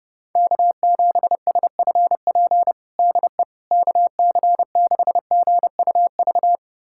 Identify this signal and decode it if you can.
Morse code, heard as a keyed tone, about 30 wpm, 700 Hz. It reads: K7SFP DE KC6GUV
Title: USA Callsign DE Callsign 30wpm